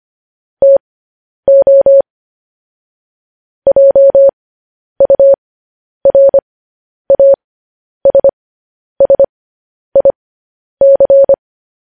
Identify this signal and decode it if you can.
Morse code, heard as a keyed tone, about 25 wpm, spaced as for 10 wpm, 550 Hz. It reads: TO JURASSIC
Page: news-headlines-in-morse-code-at-10-wpm